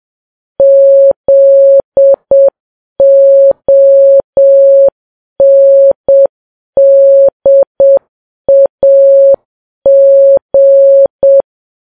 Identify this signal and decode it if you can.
Morse code, heard as a keyed tone, about 7 wpm, 550 Hz. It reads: ZONDAG
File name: Files Web tls / Morse on9fv / morse7kps